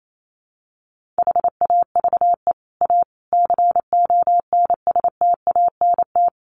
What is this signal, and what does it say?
Morse code, heard as a keyed tone, about 28 wpm, 700 Hz. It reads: HAVE A CONSTANT
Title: Sentences from Top 1500 Words 28wpm